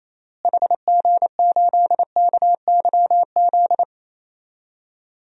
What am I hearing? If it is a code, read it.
Morse code, heard as a keyed tone, about 28 wpm, 700 Hz. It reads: HG8KYZ
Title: DXCC-100 Call Signs - 6 Characters - Repeated 2x - 28wpm